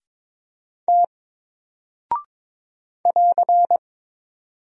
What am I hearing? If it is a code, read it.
Morse code, heard as a keyed tone, about 22 wpm, 700 Hz. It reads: T <AR>
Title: Single Character (Letter:Number:Punctuation:Prosign) Alt 22wpm